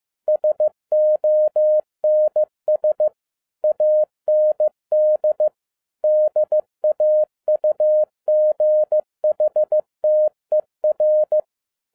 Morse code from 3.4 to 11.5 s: AND DAUGHTER